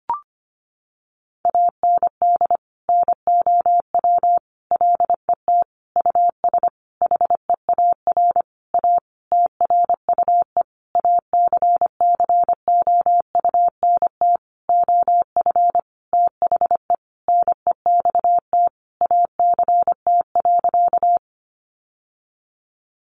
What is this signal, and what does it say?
Morse code, heard as a keyed tone, about 25 wpm, 700 Hz. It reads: AND NOW LET US HEAR A TRUE ACCOUNT OF THE NEXT ACT.
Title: Sentences from Top 900 Words 25wpm